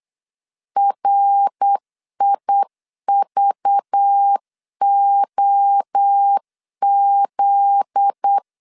Morse code, heard as a keyed tone, about 8 wpm, 800 Hz. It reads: RIVOZ
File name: LEZIONI / 8 YJ45 / 8 Y 8wpm 3245